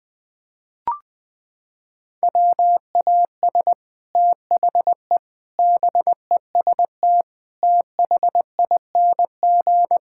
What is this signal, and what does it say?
Morse code, heard as a keyed tone, about 21 wpm, 700 Hz. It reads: WAS THE BEST THING
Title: Sets of 8 Words - 20wpm